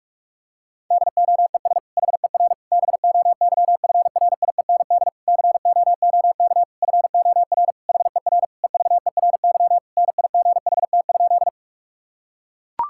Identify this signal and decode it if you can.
Morse code, heard as a keyed tone, about 41 wpm, 700 Hz. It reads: DOES HER BOYFRIEND COOK FOR HER EVERY NIGHT?
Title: Sentences from Top 1000 Words - Review Part 5 - 45wpm